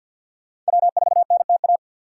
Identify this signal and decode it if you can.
Morse code, heard as a keyed tone, about 60 wpm, 700 Hz. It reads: W4NTA
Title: Speed Racing - USA Call Signs 50wpm